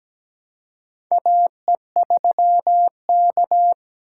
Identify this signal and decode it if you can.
Morse code, heard as a keyed tone, about 17 wpm, 700 Hz. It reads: AE3K
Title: USA Call Signs - 4 Characters - 2x Character Spacing - 17wpm